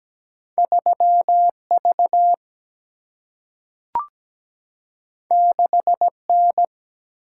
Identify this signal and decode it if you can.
Morse code, heard as a keyed tone, about 17 wpm, 700 Hz. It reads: 3V 6N
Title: Single-Letter Single-Number 17wpm